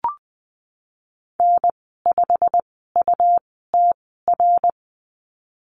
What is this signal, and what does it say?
Morse code, heard as a keyed tone, about 19 wpm, spaced as for 14 wpm, 700 Hz. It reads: N5UTR
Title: 161 Lesson 26g - Letter g - Call Signs 20wpm